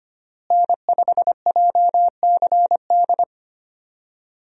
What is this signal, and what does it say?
Morse code, heard as a keyed tone, about 25 wpm, 700 Hz. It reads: N5JCD
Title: USA Call Signs - 5 Characters - Repeated 2x - 25wpm